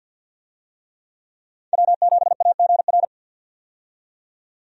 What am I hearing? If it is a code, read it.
Morse code, heard as a keyed tone, about 47 wpm, 700 Hz. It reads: W7AGR
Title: USA Call Signs - 5 Characters - 50wpm